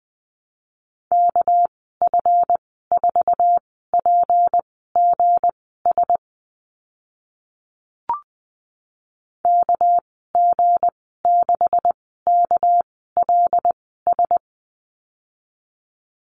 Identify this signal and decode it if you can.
Morse code, heard as a keyed tone, about 20 wpm, spaced as for 14 wpm, 700 Hz. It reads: KF4PGS KG6KLS
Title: 238 Lesson 38g - Letter q - Call Signs 20wpm